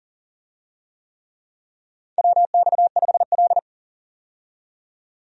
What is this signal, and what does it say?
Morse code, heard as a keyed tone, about 40 wpm, 700 Hz. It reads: WX5L
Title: USA Call Signs - 4 Characters - 40wpm